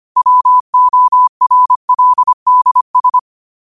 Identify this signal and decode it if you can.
Morse code, heard as a keyed tone, about 25 wpm, 1000 Hz. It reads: WORLDS